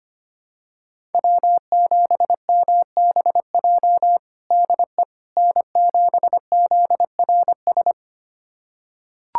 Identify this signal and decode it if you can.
Morse code, heard as a keyed tone, about 25 wpm, 700 Hz. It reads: W7MBJ DE N7ZRS
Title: USA Callsign DE Callsign - 5 Characters - 25wpm